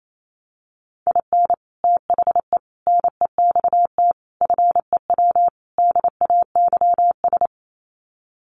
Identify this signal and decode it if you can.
Morse code, heard as a keyed tone, about 28 wpm, 700 Hz. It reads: IN THE NEXT FEW DAYS
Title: Sets of 5 Words Encore2 28wpm